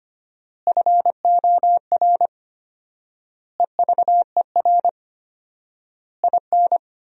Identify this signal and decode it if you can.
Morse code, heard as a keyed tone, about 25 wpm, 700 Hz. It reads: FOR EVER IN
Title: 4x Word-Spacing - Sentences from Top 400 Words - 25wpm